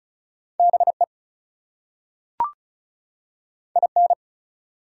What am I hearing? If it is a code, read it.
Morse code, heard as a keyed tone, about 36 wpm, 700 Hz. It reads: BE IN
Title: Two Letter Words - 35wpm